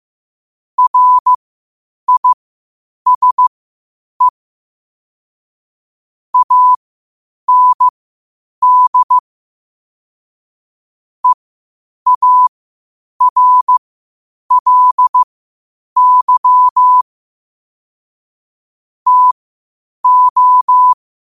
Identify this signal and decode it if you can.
Morse code, heard as a keyed tone, about 15 wpm, spaced as for 8 wpm, 1000 Hz. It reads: RISE AND EARLY TO
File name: qotdfiles / 8wpm